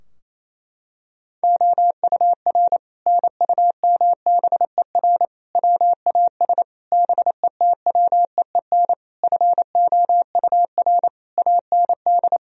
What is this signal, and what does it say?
Morse code, heard as a keyed tone, about 28 wpm, 700 Hz. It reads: OUR NUMBER WAS BETWEEN FOUR AND
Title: Sentences from Top 1000 Words - Review Part 2 - 28wpm